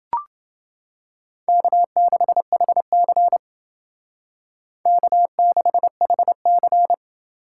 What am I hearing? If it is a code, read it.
Morse code, heard as a keyed tone, about 30 wpm, 700 Hz. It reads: K6HC K6HC